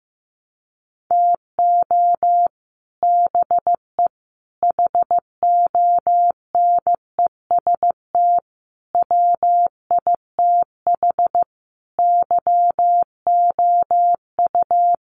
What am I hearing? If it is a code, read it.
Morse code, heard as a keyed tone, about 15 wpm, 700 Hz. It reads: TO BE HONEST WITH YOU
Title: Sets of 5 Words Encore 15wpm